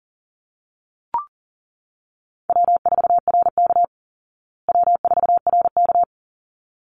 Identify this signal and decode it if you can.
Morse code, heard as a keyed tone, about 40 wpm, 700 Hz. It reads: W4RK W4RK
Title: Easy Call Signs Repeated 2x - 40wpm